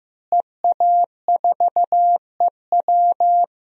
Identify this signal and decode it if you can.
Morse code, heard as a keyed tone, about 15 wpm, 700 Hz. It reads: EA4EW
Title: DXCC-100 Call Signs - 5 Characters - Repeated 2x - 15wpm